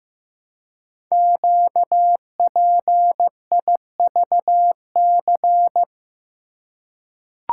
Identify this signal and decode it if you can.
Morse code, heard as a keyed tone, about 15 wpm, 700 Hz. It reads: QPIVC
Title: Code Groups - 5 Letters - 15wpm